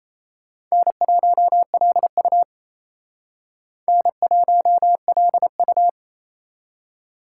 Morse code, heard as a keyed tone, about 34 wpm, 700 Hz. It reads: N1LU N1LU
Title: Speed Racing - USA 1x2 Call Signs 28wpm